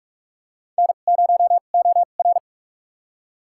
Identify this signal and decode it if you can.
Morse code, heard as a keyed tone, about 47 wpm, spaced as for 33 wpm, 700 Hz. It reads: N0OR
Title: USA Call Signs - 4 Characters - 2x Character Spacing - 45wpm